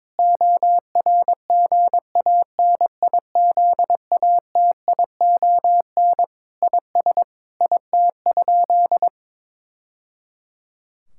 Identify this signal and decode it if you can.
Morse code, heard as a keyed tone, about 22 wpm, 700 Hz. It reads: ORGANIZATION IS IT?
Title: Sentences from Top 1000 Words - Review Part 5 - 22wpm